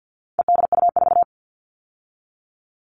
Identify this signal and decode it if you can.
Morse code, heard as a keyed tone, about 50 wpm, 700 Hz. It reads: EDU4
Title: All Cut Numbers - 4-Digit 50wpm